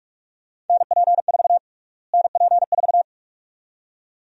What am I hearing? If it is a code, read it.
Morse code, heard as a keyed tone, about 46 wpm, 700 Hz. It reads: NP4 NP4
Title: USA Call Sign Prefixes - Repeated 2x - 45wpm